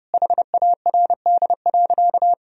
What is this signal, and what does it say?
Morse code, heard as a keyed tone, about 32 wpm, 700 Hz. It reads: HARD.
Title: Sentences from Top 1000 Words - Review Part 4 - 30wpm